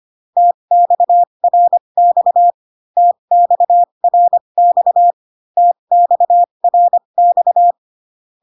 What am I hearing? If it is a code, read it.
Morse code, heard as a keyed tone, about 25 wpm, spaced as for 22 wpm, 700 Hz. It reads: TXRX TXRX TXRX
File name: CW common abbreviations alphabetical repeated 3 times 25 18wpm